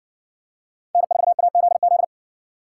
Morse code, heard as a keyed tone, about 54 wpm, 700 Hz. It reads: N4AZZ